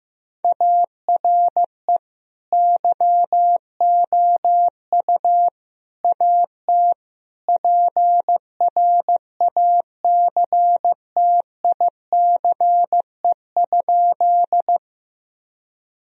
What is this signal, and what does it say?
Morse code, heard as a keyed tone, about 15 wpm, 700 Hz. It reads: ARE YOU AT PRACTICE?
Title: Sentences from Top 1000 Words - Review Part 4 - 15wpm